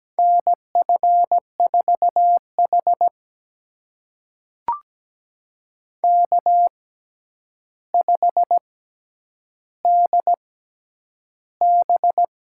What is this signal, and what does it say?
Morse code, heard as a keyed tone, about 17 wpm, spaced as for 9 wpm, 700 Hz. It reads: NF4H K5DB
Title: USA Call Signs - 4 Characters - 6x Character Spacing - 17wpm